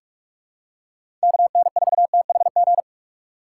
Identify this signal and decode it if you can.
Morse code, heard as a keyed tone, about 45 wpm, 700 Hz. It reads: KN4THG
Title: USA Call Signs - 6 Characters - 4x Character Spacing - 45wpm